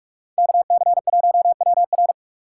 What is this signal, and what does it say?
Morse code, heard as a keyed tone, about 58 wpm, 700 Hz. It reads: KC1WR